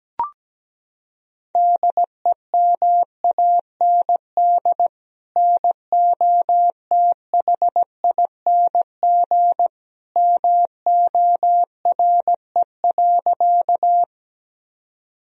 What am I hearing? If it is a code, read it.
Morse code, heard as a keyed tone, about 17 wpm, 700 Hz. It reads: DEMAND NOTHING MORE.
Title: Sentences from Top 1000 Words - Review Part 4 - 17wpm